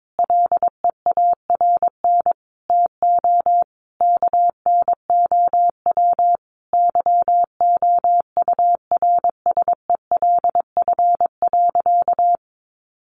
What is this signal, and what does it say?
Morse code, heard as a keyed tone, about 22 wpm, 700 Hz. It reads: LEARN TO KNOW YOURSELF.
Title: Sentences from Top 600 Words 22wpm